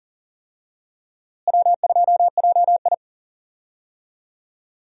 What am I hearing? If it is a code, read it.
Morse code, heard as a keyed tone, about 40 wpm, 700 Hz. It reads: W2JI
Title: USA Call Signs - 4 Characters - Repeated 2x - 40wpm